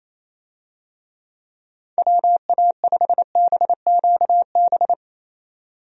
Morse code, heard as a keyed tone, about 27 wpm, 700 Hz. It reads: WA5BQB
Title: USA Call Signs - 6 Characters - 28wpm